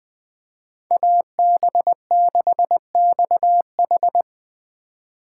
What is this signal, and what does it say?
Morse code, heard as a keyed tone, about 20 wpm, 700 Hz. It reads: AB6XH
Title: USA Call Signs - 5 Characters - 2x Character Spacing - 20wpm